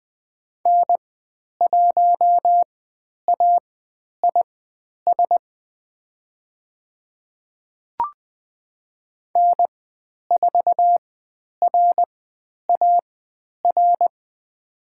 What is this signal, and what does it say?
Morse code, heard as a keyed tone, about 20 wpm, spaced as for 10 wpm, 700 Hz. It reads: N1AIS N4RAR
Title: 109 Lesson 18g - Letter m - Call Signs 20wpm